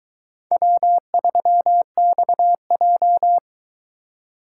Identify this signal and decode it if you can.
Morse code, heard as a keyed tone, about 24 wpm, 700 Hz. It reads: W3XJ